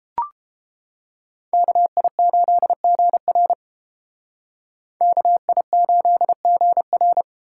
Morse code, heard as a keyed tone, about 29 wpm, 700 Hz. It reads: KI8GR KI8GR